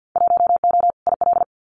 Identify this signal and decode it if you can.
Morse code, heard as a keyed tone, about 50 wpm, 700 Hz. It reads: 1O IL
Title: Winter Field Day Exchanges 50wpm